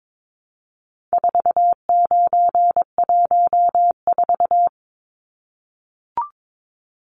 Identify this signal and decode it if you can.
Morse code, heard as a keyed tone, about 22 wpm, 700 Hz. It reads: 4914
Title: Numbers - 4-Digit 22wpm